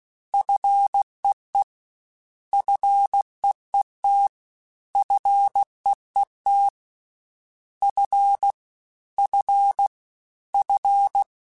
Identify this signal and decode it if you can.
Morse code, heard as a keyed tone, about 16 wpm, 800 Hz. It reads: FEE FEET FEET F F F
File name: Combinations / Combinations-mp3s / Combinations@15